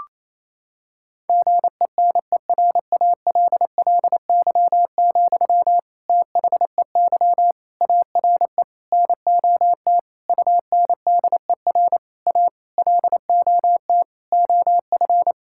Morse code, heard as a keyed tone, about 27 wpm, 700 Hz. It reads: GENERALLY, THEY ARE NOT UNDER A LOT OF
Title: Sentences from Top 1000 Words - Review Part 6 - 28wpm